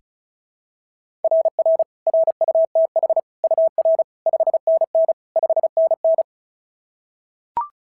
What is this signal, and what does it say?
Morse code, heard as a keyed tone, about 33 wpm, 660 Hz. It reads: RR RUTH UR 5NN 5NN
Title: SOTA QSO Practice 35wpm